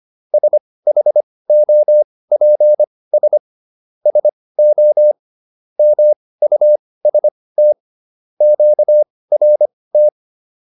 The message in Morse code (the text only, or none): SHOPS SO MUST QRT